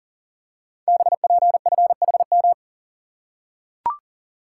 Morse code, heard as a keyed tone, about 42 wpm, 690 Hz. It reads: BPFHM
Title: Code Groups - 5 Letters - 40wpm